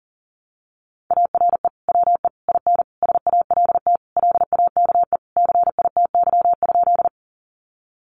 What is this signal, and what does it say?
Morse code, heard as a keyed tone, about 40 wpm, 700 Hz. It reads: ARE WE IN SALT LAKE CITY?
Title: Sentences from Top 1100 Words 40wpm